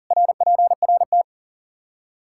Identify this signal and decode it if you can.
Morse code, heard as a keyed tone, about 40 wpm, 700 Hz. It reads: RPRT
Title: K1USN SST QSO Practice 40wpm